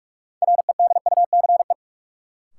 Morse code, heard as a keyed tone, about 42 wpm, 700 Hz. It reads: REDUCE
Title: Sentences from Top 1100 Words - New Word Drill Alt - 45wpm